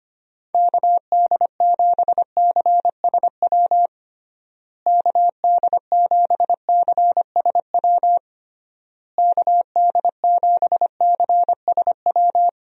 Morse code, heard as a keyed tone, about 25 wpm, 700 Hz. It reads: KD7CSW KD7CSW KD7CSW
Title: USA Call Signs - 6 Characters - Repeated 3x - 25wpm